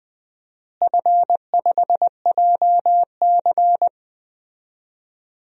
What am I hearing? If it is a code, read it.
Morse code, heard as a keyed tone, about 20 wpm, 700 Hz. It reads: F5JC
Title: DXCC-100 Call Signs - 4 Characters - Repeated 3x - 20wpm